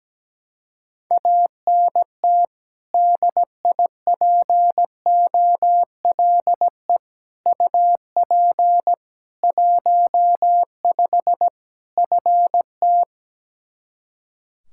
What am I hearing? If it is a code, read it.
Morse code, heard as a keyed tone, about 17 wpm, 700 Hz. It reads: ANT DIPOLE UP 15 FT
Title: Practice QSO - Exchange 2 - Info - 17wpm